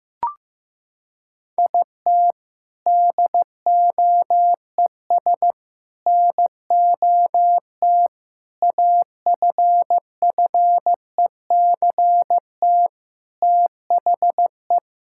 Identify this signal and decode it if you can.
Morse code, heard as a keyed tone, about 15 wpm, 700 Hz. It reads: IT DOES NOT AFFECT THE
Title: Sentences from Top 1000 Words - Review Part 6 - 15wpm